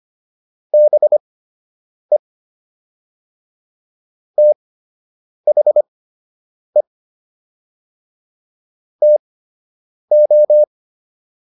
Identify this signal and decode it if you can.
Morse code, heard as a keyed tone, about 25 wpm, spaced as for 8 wpm, 600 Hz. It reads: BE THE TO